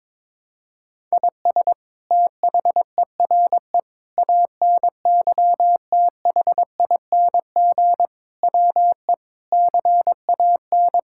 Morse code, heard as a keyed tone, about 22 wpm, 700 Hz. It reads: IS THERE ANYTHING WE CAN
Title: Sentences from Top 1000 Words - Review Part 6 - 22wpm